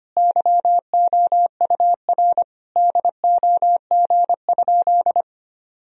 Morse code, heard as a keyed tone, about 25 wpm, 700 Hz. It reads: YOUR DOG?
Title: Sentences from Top 1000 Words - Review Part 2 - 25wpm